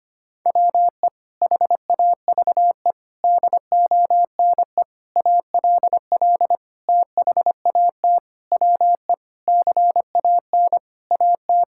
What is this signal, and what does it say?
Morse code, heard as a keyed tone, about 25 wpm, 700 Hz. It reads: WE HAVE DONE ALL THAT WE CAN AT
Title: Sentences from Top 1000 Words - Review Part 5 - 25wpm